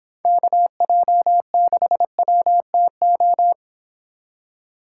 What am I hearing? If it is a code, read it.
Morse code, heard as a keyed tone, about 31 wpm, 700 Hz. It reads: KJ6WTO